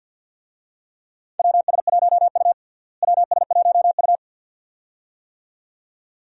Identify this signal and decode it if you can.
Morse code, heard as a keyed tone, about 51 wpm, 700 Hz. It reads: WS1U WS1U
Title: USA Call Signs - 4 Characters - Repeated 2x - 50wpm